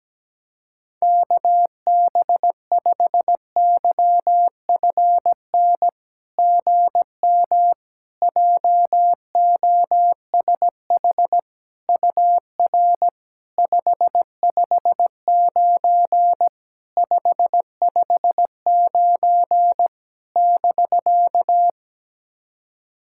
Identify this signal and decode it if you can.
Morse code, heard as a keyed tone, about 17 wpm, 700 Hz. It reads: KB5YFN GM JOSH UR 559 559 <BK>
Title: SOTA QSO Practice 17wpm